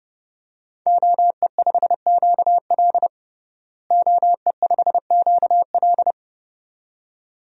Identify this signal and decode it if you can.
Morse code, heard as a keyed tone, about 31 wpm, 700 Hz. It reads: OE5QL OE5QL
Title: DXCC-100 Call Signs - 5 Characters - Repeated 2x - 30wpm